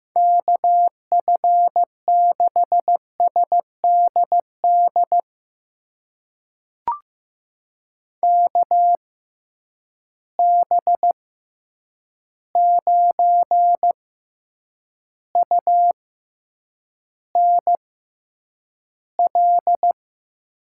Morse code, heard as a keyed tone, about 15 wpm, spaced as for 8 wpm, 700 Hz. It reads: KF6SDD KB9UNL
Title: USA Call Signs - 6 Characters - 6x Character Spacing - 15wpm